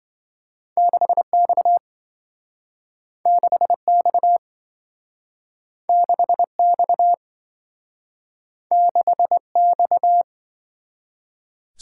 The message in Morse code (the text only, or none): 6X 6X 6X 6X